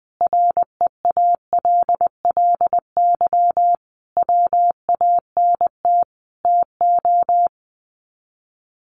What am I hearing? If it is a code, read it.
Morse code, heard as a keyed tone, about 20 wpm, 700 Hz. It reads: REALLY WANT TO
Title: Sets of 5 Words Encore2 20wpm